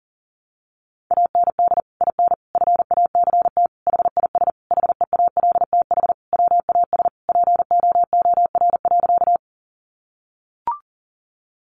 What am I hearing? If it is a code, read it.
Morse code, heard as a keyed tone, about 40 wpm, 700 Hz. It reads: AND IN FACT HIS HEALTH WAS POOR.
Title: Sentences from Top 1000 Words 40wpm